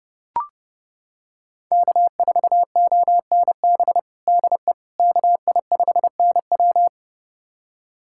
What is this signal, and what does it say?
Morse code, heard as a keyed tone, about 30 wpm, 700 Hz. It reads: K4ONB DE KI5NW
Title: USA Callsign DE Callsign 30wpm